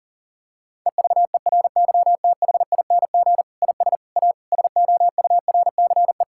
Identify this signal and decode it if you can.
Morse code, heard as a keyed tone, about 40 wpm, 700 Hz. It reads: EVERYTHING IS A SOURCE
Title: Sentences from Top 1000 Words - Review Part 5 - 40wpm